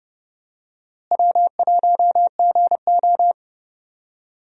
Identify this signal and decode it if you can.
Morse code, heard as a keyed tone, about 30 wpm, 700 Hz. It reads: W1GO
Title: USA Call Signs - 4 Characters - Repeated 3x - 30wpm